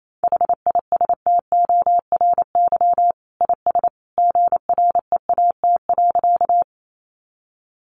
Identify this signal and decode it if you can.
Morse code, heard as a keyed tone, about 28 wpm, 700 Hz. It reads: HISTORY IS GREAT.
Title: Sentences from Top 600 Words 28wpm